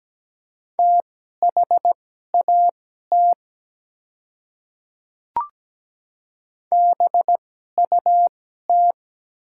Morse code, heard as a keyed tone, about 17 wpm, spaced as for 12 wpm, 700 Hz. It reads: THAT BUT
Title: 2x Character Spacing - Top 200 Words - 17wpm